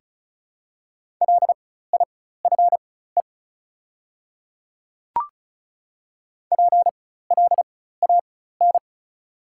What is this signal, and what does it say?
Morse code, heard as a keyed tone, about 34 wpm, spaced as for 16 wpm, 690 Hz. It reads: LIFE PLAN
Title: 4x Character Spacing - Top 500 Words - 35wpm